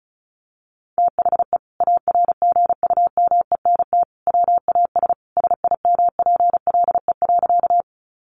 Morse code, heard as a keyed tone, about 34 wpm, 700 Hz. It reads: THE ARGUMENT WAS SIMPLE.
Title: Sentences from Top 1400 Words 35wpm